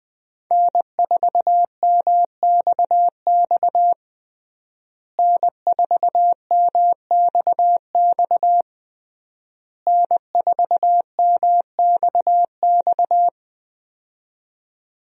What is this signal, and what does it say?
Morse code, heard as a keyed tone, about 20 wpm, 700 Hz. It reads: N4MXX N4MXX N4MXX
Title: USA Call Signs - 5 Characters - Repeated 3x - 20wpm